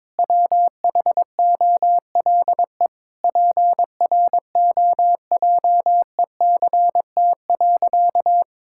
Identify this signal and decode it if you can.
Morse code, heard as a keyed tone, about 21 wpm, 700 Hz. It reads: WHOLE PROJECT.
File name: Sentences with 7-Word Phrases - 22wpm